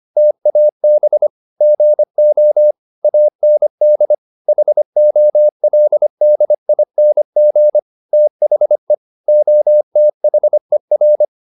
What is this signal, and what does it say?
Morse code, heard as a keyed tone, about 25 wpm, 600 Hz. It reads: TAB GO AND HOLDING THE OTHER